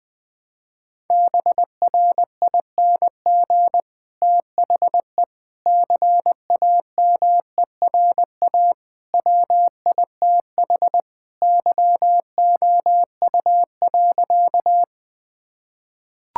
Morse code, heard as a keyed tone, about 20 wpm, 700 Hz. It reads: BRING THE CAMERA WITH YOU.
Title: Sentences from Top 1000 Words - Review Part 5 - 20wpm